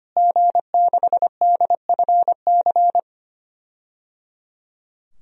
Morse code, heard as a keyed tone, about 25 wpm, 700 Hz. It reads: G6DFC
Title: DXCC-100 Call Signs - 5 Characters - Repeated 3x - 25wpm